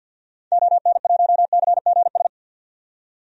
Morse code, heard as a keyed tone, about 52 wpm, 700 Hz. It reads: ON1CGS
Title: DXCC-100 Call Signs - 6 Characters - 50wpm